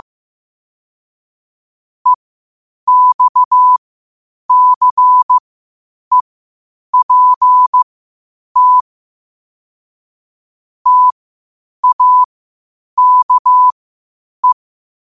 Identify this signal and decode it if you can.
Morse code, heard as a keyed tone, about 15 wpm, spaced as for 8 wpm, 1000 Hz. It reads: EXCEPT TAKE
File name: 8wpm